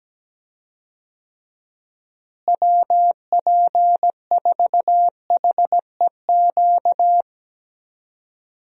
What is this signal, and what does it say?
Morse code, heard as a keyed tone, about 17 wpm, 700 Hz. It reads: WP4HEQ
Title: USA Call Signs - 6 Characters - Repeated 2x - 17wpm